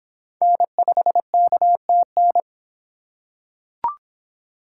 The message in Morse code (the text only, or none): N5KTN